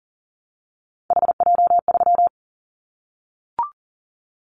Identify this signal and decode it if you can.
Morse code, heard as a keyed tone, about 40 wpm, 700 Hz. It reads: HJ3
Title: Callsign Prefixes for Top 100 DXCC - 40wpm